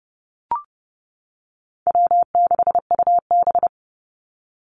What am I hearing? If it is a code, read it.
Morse code, heard as a keyed tone, about 30 wpm, 700 Hz. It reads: W6UB
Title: NA SOTA Activator Callsigns 30wpm